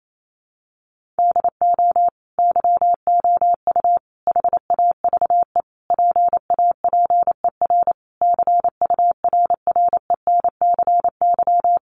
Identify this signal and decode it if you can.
Morse code, heard as a keyed tone, about 28 wpm, 700 Hz. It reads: DO YOU HAVE PAPER CURRENCY
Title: Sentences from Top 1500 Words 28wpm